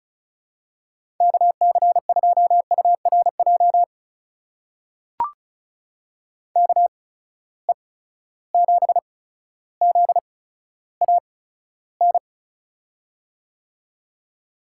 Morse code, heard as a keyed tone, about 36 wpm, spaced as for 15 wpm, 700 Hz. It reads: KC2URJ KE7ZAN
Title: USA Call Signs - 6 Characters - 8x Character Spacing - 35wpm